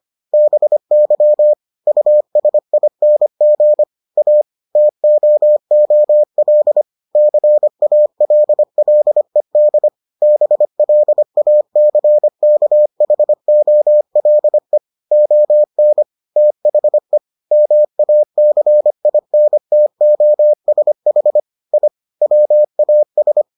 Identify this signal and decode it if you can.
Morse code, heard as a keyed tone, about 25 wpm, 600 Hz. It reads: BY USING A TOOL CALLED BLACKHOLE ON THE MACINTOSH I WAS